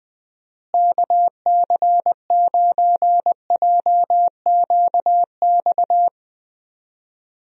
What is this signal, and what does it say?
Morse code, heard as a keyed tone, about 20 wpm, 700 Hz. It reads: KC9JQX
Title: USA Call Signs - 6 Characters - Repeated 3x - 20wpm